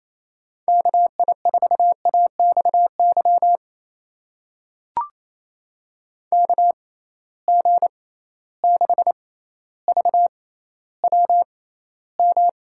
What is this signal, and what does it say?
Morse code, heard as a keyed tone, about 27 wpm, spaced as for 14 wpm, 700 Hz. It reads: KI4AXY KG6VWM
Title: USA Call Signs - 6 Characters - 6x Character Spacing - 28wpm